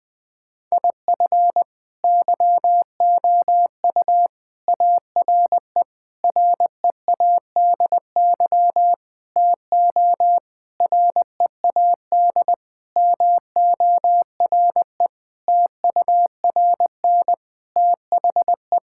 Morse code, heard as a keyed tone, about 20 wpm, 700 Hz. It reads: IF YOU ARE READY TO READ MORE TURN THE
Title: Sentences with 6-Word Phrases - 20wpm